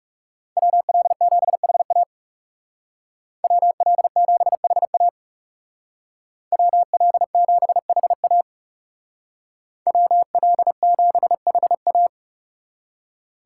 Morse code, heard as a keyed tone, about 37 wpm, 700 Hz. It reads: WL7HA WL7HA WL7HA WL7HA
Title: Speed Racing - USA 2x2 Call Signs 30wpm